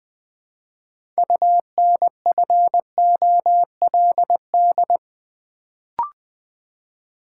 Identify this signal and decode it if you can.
Morse code, heard as a keyed tone, about 20 wpm, 700 Hz. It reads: UNFOLD
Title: Build A Word - 2 Letters At A Time - 6-Letter Words - 20wpm